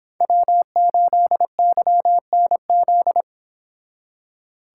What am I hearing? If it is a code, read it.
Morse code, heard as a keyed tone, about 26 wpm, 700 Hz. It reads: W8YNZ